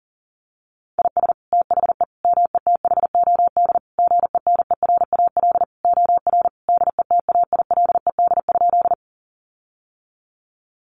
Morse code, heard as a keyed tone, about 40 wpm, 700 Hz. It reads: IS THE METHOD GENERAL OR DETAILED?
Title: Sentences from Top 1500 Words 40wpm